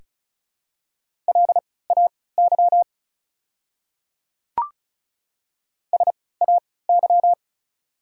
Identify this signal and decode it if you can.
Morse code, heard as a keyed tone, about 34 wpm, spaced as for 20 wpm, 690 Hz. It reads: LAY SAY
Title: 3x Character Spacing - Top 500 Words - 35wpm